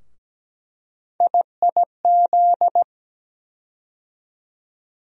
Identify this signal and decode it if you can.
Morse code, heard as a keyed tone, about 17 wpm, 700 Hz. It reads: IIZ
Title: USA Call Sign Postfixes - 3 Letters 17wpm